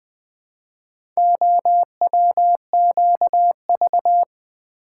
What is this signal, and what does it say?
Morse code, heard as a keyed tone, about 20 wpm, 700 Hz. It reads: OWQV
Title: Four Letter-Number Combinations - 20wpm